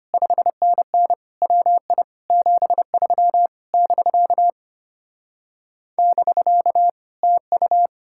Morse code, heard as a keyed tone, about 33 wpm, 700 Hz. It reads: 5NN WI 73 <BK> <BK> TU